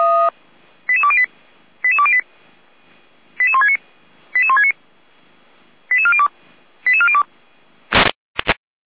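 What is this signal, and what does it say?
Five-tone selective call (ZVEI), heard as quick runs of five tones, tones of about 80 ms. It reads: 88208 88208 88169 88169 88402 88402
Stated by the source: ZVEI five-tone call received by SDR, from the audio.f85360.feuerwehr-orlamuende.org fire brigade (Feuerwehr Orlamünde) server.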